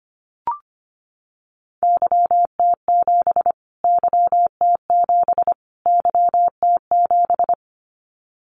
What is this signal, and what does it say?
Morse code, heard as a keyed tone, about 25 wpm, 700 Hz. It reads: YT7 YT7 YT7
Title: Callsign Prefixes for Top 25 DXCC - Repeated 3x - 25wpm